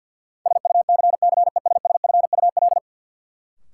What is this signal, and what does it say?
Morse code, heard as a keyed tone, about 50 wpm, 700 Hz. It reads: SUCCESSFUL
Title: Sentences from Top 900 Words - New Word Drill - 50wpm